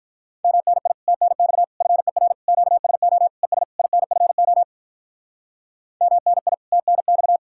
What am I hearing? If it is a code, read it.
Morse code, heard as a keyed tone, about 42 wpm, 700 Hz. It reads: MNI TNX FER QSO ES INFO MNI TNX
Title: Speed-Racing Common QSO Phrases Encore - 35wpm